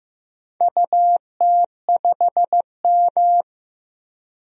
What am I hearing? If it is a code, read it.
Morse code, heard as a keyed tone, about 15 wpm, 700 Hz. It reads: UT5M
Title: DXCC-100 Call Signs - 4 Characters - Repeated 3x - 15wpm